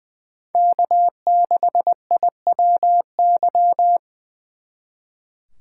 Morse code, heard as a keyed tone, about 23 wpm, 700 Hz. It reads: K6IWY
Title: Speed Racing - USA 1x3 Call Signs 20wpm